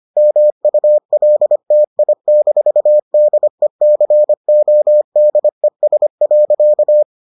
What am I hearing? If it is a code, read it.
Morse code, heard as a keyed tone, about 25 wpm, 600 Hz. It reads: MULTI-DECODES.